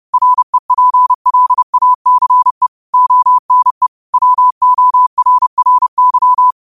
Morse code, heard as a keyed tone, about 30 wpm, 1000 Hz. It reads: REPLACE ONE WORRY
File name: qotdfiles / 30wpm